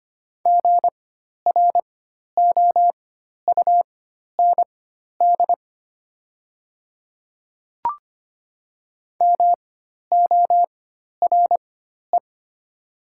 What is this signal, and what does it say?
Morse code, heard as a keyed tone, about 25 wpm, spaced as for 12 wpm, 700 Hz. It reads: GROUND MORE
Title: 4x Character Spacing - Top 500 Words - 25wpm